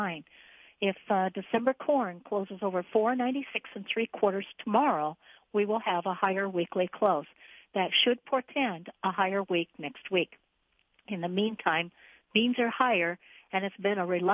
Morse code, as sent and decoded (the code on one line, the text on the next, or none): none